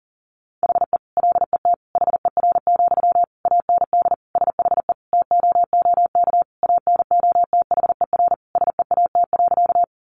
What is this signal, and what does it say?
Morse code, heard as a keyed tone, about 40 wpm, 700 Hz. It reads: HE LET HER, AND SHE TOOK ANOTHER SEAT.
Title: Sentences from Top 1500 Words 40wpm